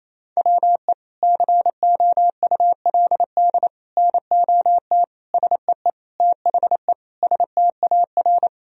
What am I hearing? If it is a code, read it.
Morse code, heard as a keyed tone, about 28 wpm, 700 Hz. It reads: WE COULD NOT SEE THE STAR
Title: Sentences from Top 1000 Words - Review Part 2 - 28wpm